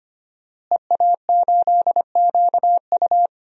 Morse code, heard as a keyed tone, about 25 wpm, 700 Hz. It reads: EA8QU
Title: DXCC-100 Call Signs - 5 Characters - Repeated 2x - 25wpm